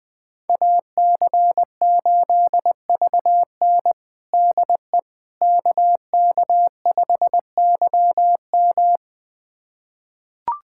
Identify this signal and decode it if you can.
Morse code, heard as a keyed tone, about 20 wpm, 700 Hz. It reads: AC8VN DE KK5YM
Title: USA Callsign DE Callsign - 5 Characters - 20wpm